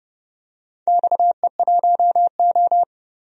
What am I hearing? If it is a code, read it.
Morse code, heard as a keyed tone, about 29 wpm, 700 Hz. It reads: XE1O
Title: DXCC-100 Call Signs - 4 Characters - 30wpm